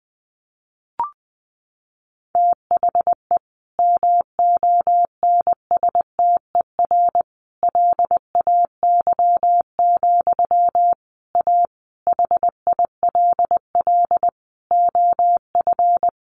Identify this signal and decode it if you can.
Morse code, heard as a keyed tone, about 19 wpm, 700 Hz. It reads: THE MONSTER LAY, A HILL OF
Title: A Sound Of Thunder 20wpm